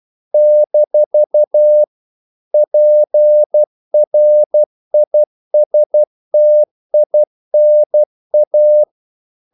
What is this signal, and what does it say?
Morse code, heard as a keyed tone, about 12 wpm, 600 Hz. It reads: - PRISTINA
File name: CW / Practice / NationalCapitals12